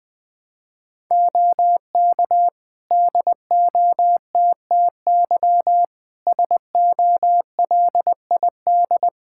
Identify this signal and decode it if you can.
Morse code, heard as a keyed tone, about 20 wpm, 700 Hz. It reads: OK DOTTY SOLID
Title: K1USN SST QSO Practice 20wpm